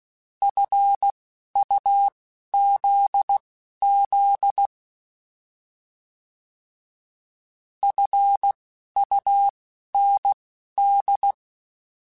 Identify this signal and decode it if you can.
Morse code, heard as a keyed tone, about 16 wpm, spaced as for 11 wpm, 800 Hz. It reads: FUZZ FUND
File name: Alphabetical / Alphabetical mp3s / Fs @10